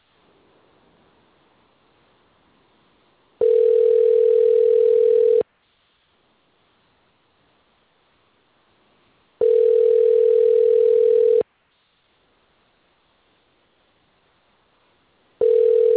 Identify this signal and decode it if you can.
Ringback tone (North America), 440 + 480 Hz.